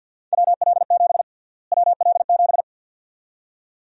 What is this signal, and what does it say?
Morse code, heard as a keyed tone, about 52 wpm, 700 Hz. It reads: WL7 WL7
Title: USA Call Sign Prefixes - Repeated 2x - 50wpm